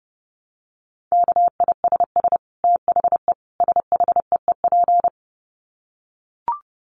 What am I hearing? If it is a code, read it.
Morse code, heard as a keyed tone, about 30 wpm, 700 Hz. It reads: KISS THE SHEEP
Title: Dit Challenge 30wpm